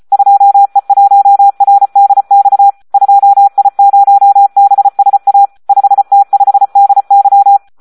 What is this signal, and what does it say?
Morse code, heard as a keyed tone, about 34 wpm, 800 Hz. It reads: 2E1RDX 2I0BSA 5T5DY
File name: Audio practica CW / SD 1 34